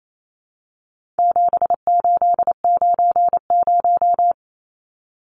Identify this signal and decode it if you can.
Morse code, heard as a keyed tone, about 28 wpm, 700 Hz. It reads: 7890
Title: Numbers - 4-Digit 28wpm